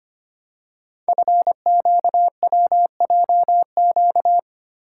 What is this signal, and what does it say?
Morse code, heard as a keyed tone, about 25 wpm, 700 Hz. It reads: FQWJQ
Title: Code Groups - 5 Letters - 25wpm